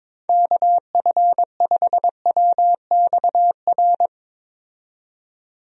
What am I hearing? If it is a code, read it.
Morse code, heard as a keyed tone, about 22 wpm, 700 Hz. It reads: KF5WXR
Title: USA Call Signs - 6 Characters - 22wpm